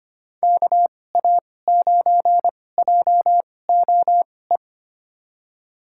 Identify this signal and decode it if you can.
Morse code, heard as a keyed tone, about 25 wpm, spaced as for 18 wpm, 700 Hz. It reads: KA9JOE
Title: USA Call Signs - 6 Characters - 2x Character Spacing - 25wpm